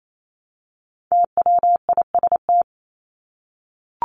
Morse code, heard as a keyed tone, about 28 wpm, 700 Hz. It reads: TWIST
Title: Top 1751-2000 Words 28wpm